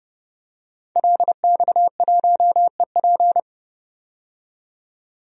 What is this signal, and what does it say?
Morse code, heard as a keyed tone, about 30 wpm, 700 Hz. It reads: LX1EP
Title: DXCC-100 Call Signs - 5 Characters - 30wpm